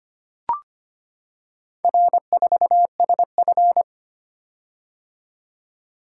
Morse code, heard as a keyed tone, about 25 wpm, 690 Hz. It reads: R4SF
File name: practice / 25wpm / Four Letter-Number Combinations - 25wpm